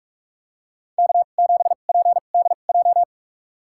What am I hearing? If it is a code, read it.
Morse code, heard as a keyed tone, about 44 wpm, spaced as for 32 wpm, 700 Hz. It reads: K7PDJ
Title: USA Call Signs - 5 Characters - 2x Character Spacing - 45wpm